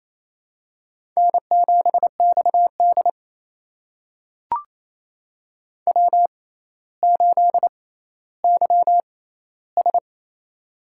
Morse code, heard as a keyed tone, about 29 wpm, spaced as for 15 wpm, 700 Hz. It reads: N7XD W8YS
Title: USA Call Signs - 4 Characters - 6x Character Spacing - 28wpm